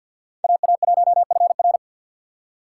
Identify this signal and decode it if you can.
Morse code, heard as a keyed tone, about 47 wpm, 700 Hz. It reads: AA1FR
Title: USA Call Signs - 5 Characters - 2x Character Spacing - 50wpm